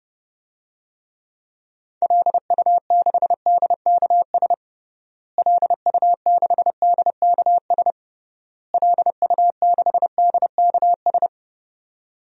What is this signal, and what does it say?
Morse code, heard as a keyed tone, about 30 wpm, 700 Hz. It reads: LU6DKS LU6DKS LU6DKS
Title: DXCC-100 Call Signs - 6 Characters - Repeated 3x - 30wpm